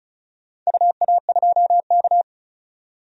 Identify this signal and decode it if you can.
Morse code, heard as a keyed tone, about 36 wpm, 700 Hz. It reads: UA2K
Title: DXCC-100 Call Signs - 4 Characters - 35wpm